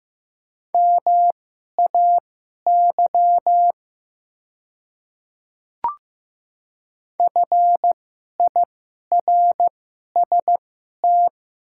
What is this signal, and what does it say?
Morse code, heard as a keyed tone, about 15 wpm, spaced as for 11 wpm, 700 Hz. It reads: MAY FIRST
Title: 2x Character Spacing - Top 200 Words - 15wpm